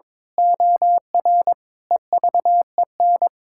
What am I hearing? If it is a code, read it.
Morse code, heard as a keyed tone, about 22 wpm, 700 Hz. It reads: OR EVEN
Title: Pangram Challenge v2 - 22wpm